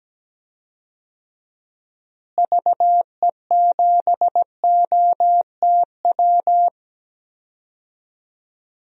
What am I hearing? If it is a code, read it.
Morse code, heard as a keyed tone, about 17 wpm, 700 Hz. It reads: VE7OTW
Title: DXCC-100 Call Signs - 6 Characters - Repeated 3x - 17wpm